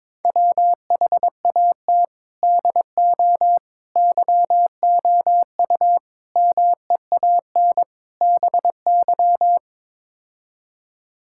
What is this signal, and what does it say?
Morse code, heard as a keyed tone, about 22 wpm, 700 Hz. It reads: WHAT DO YOU MEAN BY
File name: practice / 22wpm / Build A Phrase - 5-Word Phrases - 22wpm